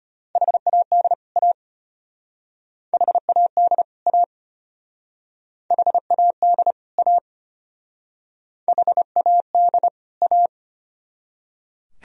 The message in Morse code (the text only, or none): HAD A HAD A HAD A HAD A